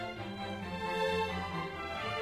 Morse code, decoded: none